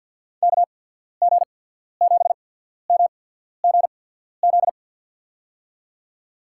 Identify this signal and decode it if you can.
Morse code, heard as a keyed tone, about 53 wpm, spaced as for 14 wpm, 700 Hz. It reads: KG7MGZ
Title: USA Call Signs - 6 Characters - 8x Character Spacing - 50wpm